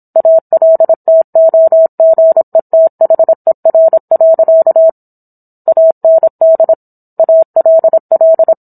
Morse code, heard as a keyed tone, about 27 wpm, 650 Hz. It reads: ALTOGETHER. AND ALL